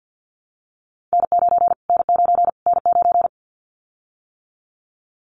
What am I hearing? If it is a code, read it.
Morse code, heard as a keyed tone, about 52 wpm, 700 Hz. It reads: N9 N9 N9
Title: Callsign Prefixes for Top 25 DXCC - Repeated 3x - 50wpm